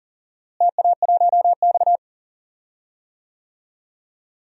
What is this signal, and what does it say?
Morse code, heard as a keyed tone, about 40 wpm, 700 Hz. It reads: TA1X
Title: DXCC-100 Call Signs - 4 Characters - Repeated 2x - 40wpm